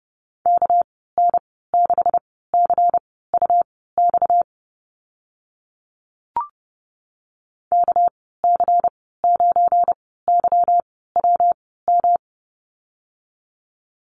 Morse code, heard as a keyed tone, about 30 wpm, spaced as for 17 wpm, 700 Hz. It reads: KN6CUX KC9YWM
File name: Lesson 37g - Letter x - Call Signs 30wpm